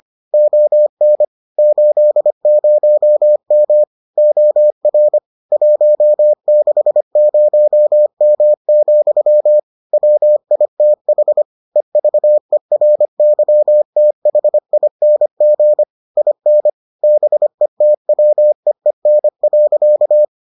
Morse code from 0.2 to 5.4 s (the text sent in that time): ON 80M OR